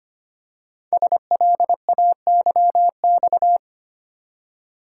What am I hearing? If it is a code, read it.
Morse code, heard as a keyed tone, about 25 wpm, 690 Hz. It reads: SLAYX
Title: Code Groups - 5 Letters - 25wpm